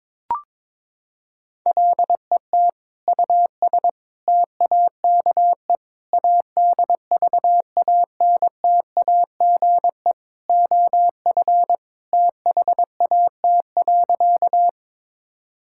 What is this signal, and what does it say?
Morse code, heard as a keyed tone, about 22 wpm, 700 Hz. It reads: LET US TAKE ADVANTAGE OF THAT.
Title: Sentences from Top 1000 Words - Review Part 6 - 22wpm